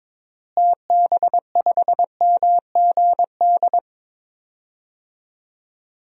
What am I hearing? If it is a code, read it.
Morse code, heard as a keyed tone, about 22 wpm, 700 Hz. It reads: TB5MGD
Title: DXCC-100 Call Signs - 6 Characters - Repeated 2x - 22wpm